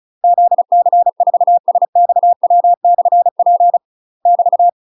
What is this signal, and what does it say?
Morse code, heard as a keyed tone, about 34 wpm, 700 Hz. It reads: ZC4SXW/P <BT>
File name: UK MASTER QSOs TRAINING/35WPM